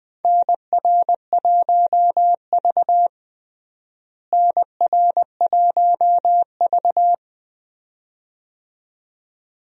Morse code, heard as a keyed tone, about 21 wpm, 700 Hz. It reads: NR1V NR1V
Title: USA Call Signs - 4 Characters - Repeated 2x - 20wpm